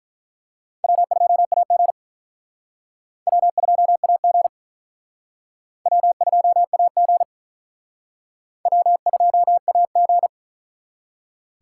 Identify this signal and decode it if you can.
Morse code, heard as a keyed tone, about 44 wpm, 700 Hz. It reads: W2AG W2AG W2AG W2AG